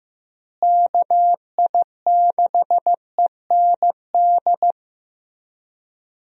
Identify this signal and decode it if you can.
Morse code, heard as a keyed tone, about 15 wpm, 700 Hz. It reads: KI6END
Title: USA Call Signs - 6 Characters - Repeated 3x - 15wpm